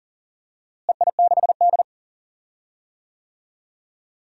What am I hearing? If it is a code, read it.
Morse code, heard as a keyed tone, about 40 wpm, 700 Hz. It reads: EI6D
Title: DXCC-100 Call Signs - 4 Characters - Repeated 2x - 40wpm